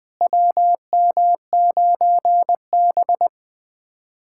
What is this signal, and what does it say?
Morse code, heard as a keyed tone, about 20 wpm, 700 Hz. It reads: WM9B
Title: USA Callsign DE Callsign - 4 Characters - 20wpm